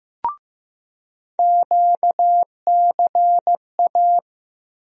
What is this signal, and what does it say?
Morse code, heard as a keyed tone, about 15 wpm, 700 Hz. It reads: QCA
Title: USA Call Sign Postfixes - 3 Letters 15wpm